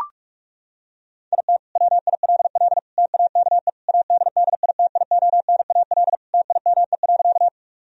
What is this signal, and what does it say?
Morse code, heard as a keyed tone, about 42 wpm, 700 Hz. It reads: IT WILL TAKE ADDITIONAL TIME.
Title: Sentences from Top 1000 Words - Review Part 4 - 45wpm